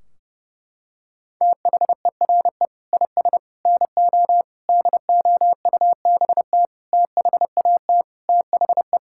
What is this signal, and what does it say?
Morse code, heard as a keyed tone, about 32 wpm, 700 Hz. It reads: THERE IS NO DOUBT THAT THE
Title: Sentences from Top 1600 Words 30wpm